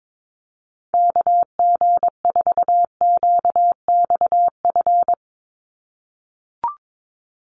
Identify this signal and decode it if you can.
Morse code, heard as a keyed tone, about 22 wpm, 700 Hz. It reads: KG4QXF
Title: USA Call Signs 22wpm